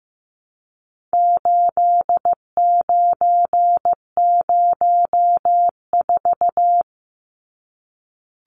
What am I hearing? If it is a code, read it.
Morse code, heard as a keyed tone, about 15 wpm, 700 Hz. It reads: 8904
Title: Numbers - 4-Digit 15wpm